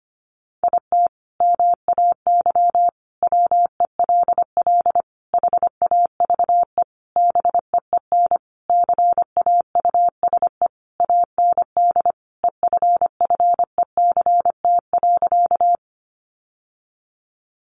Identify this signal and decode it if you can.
Morse code, heard as a keyed tone, about 25 wpm, 700 Hz. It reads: IT MAY WELL HAVE BEEN CAUSE AND EFFECT.
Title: Sentences from Top 1000 Words 25wpm